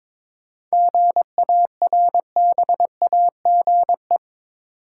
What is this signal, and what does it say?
Morse code, heard as a keyed tone, about 22 wpm, 700 Hz. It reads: GARBAGE
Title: Seven Letter Words - 22wpm